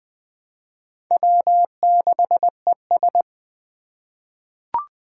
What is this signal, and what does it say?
Morse code, heard as a keyed tone, about 20 wpm, 700 Hz. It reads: W6ES
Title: Four Challenging Letter-Number Combinations 20wpm